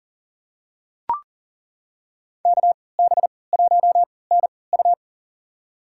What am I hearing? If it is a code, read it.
Morse code, heard as a keyed tone, about 40 wpm, spaced as for 23 wpm, 690 Hz. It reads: KB1NU
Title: USA Call Signs - 3x Character Spacing - 40wpm